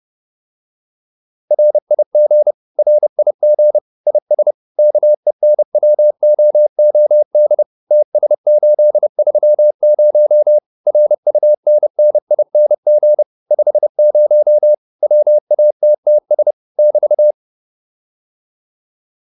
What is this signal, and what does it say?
Morse code, heard as a keyed tone, about 30 wpm, 600 Hz. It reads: RIG RIG IS KENWOOD TS830 RUNNING 50 WATTS <BT>
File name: QSO406 30